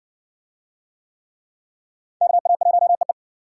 Morse code, heard as a keyed tone, about 59 wpm, 700 Hz. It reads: KA1EE